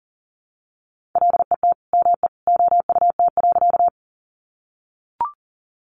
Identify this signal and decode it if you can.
Morse code, heard as a keyed tone, about 40 wpm, 700 Hz. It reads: LET ME OUT.
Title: Sentences from Top 300 Words 40wpm